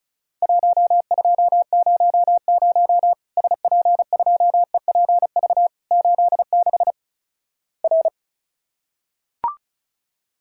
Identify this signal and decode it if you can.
Morse code, heard as a keyed tone, about 36 wpm, 700 Hz. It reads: 1200 SP2EPV 86 R
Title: Worked All Europe DX Contest - QTC Exchange - 35wpm